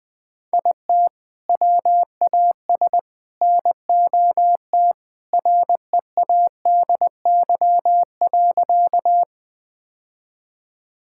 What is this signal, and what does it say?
Morse code, heard as a keyed tone, about 20 wpm, 700 Hz. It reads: IT WAS NOT READY.
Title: Sentences with 2-Word Phrases - Spoken Beginning - 20wpm